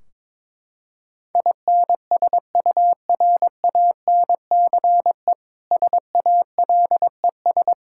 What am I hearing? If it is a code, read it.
Morse code, heard as a keyed tone, about 22 wpm, 700 Hz. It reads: INSURANCE SALES
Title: Practice QSO - Exchange 3 - More Info - 22wpm